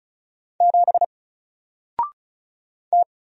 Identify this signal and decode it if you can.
Morse code, heard as a keyed tone, about 36 wpm, 700 Hz. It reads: 7 T
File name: Single-Letter-Number-35wpm